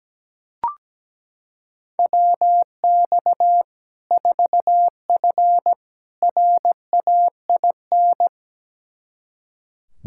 Morse code, heard as a keyed tone, about 17 wpm, 700 Hz. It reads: WX 4F RAIN
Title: Practice QSO - Exchange 2 - Info - 17wpm